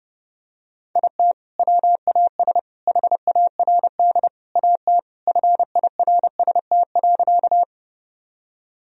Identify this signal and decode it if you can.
Morse code, heard as a keyed tone, about 30 wpm, 700 Hz. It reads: IT WAS HARD AT FIRST.
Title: Sentences from Top 1000 Words - Review Part 1 - 30wpm